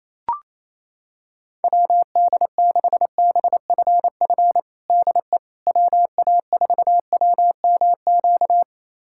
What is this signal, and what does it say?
Morse code, heard as a keyed tone, about 28 wpm, 700 Hz. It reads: WD6BFF DE WA4WMQ
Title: USA Callsign DE Callsign - 6 Characters - 28wpm